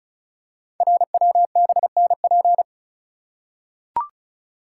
Morse code, heard as a keyed tone, about 36 wpm, 700 Hz. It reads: RWBNP
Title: Code Groups - 5 Letters - 35wpm